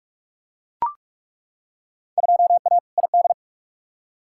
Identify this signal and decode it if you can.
Morse code, heard as a keyed tone, about 42 wpm, 700 Hz. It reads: 2A ID
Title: Field Day Exchanges Encore - 45wpm